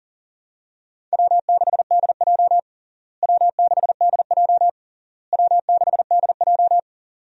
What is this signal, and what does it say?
Morse code, heard as a keyed tone, about 40 wpm, 700 Hz. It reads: W6DJ W6DJ W6DJ
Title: USA Call Signs - 4 Characters - Repeated 3x - 40wpm